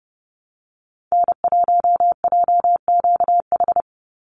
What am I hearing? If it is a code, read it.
Morse code, heard as a keyed tone, about 30 wpm, 700 Hz. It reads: N1JQH
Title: USA Call Signs 30wpm